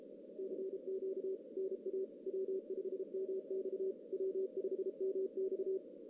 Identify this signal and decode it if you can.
Morse code, heard as a keyed tone, about 32 wpm, 380 Hz. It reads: CQ NA W5MX W5MX
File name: 2016 JAN CW NAQP / 2016 Jan naqp k5ot - 0-2